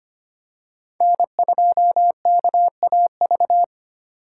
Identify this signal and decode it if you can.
Morse code, heard as a keyed tone, about 25 wpm, 700 Hz. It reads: N2KAV
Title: USA Call Signs - 5 Characters - 3x Character Spacing - 25wpm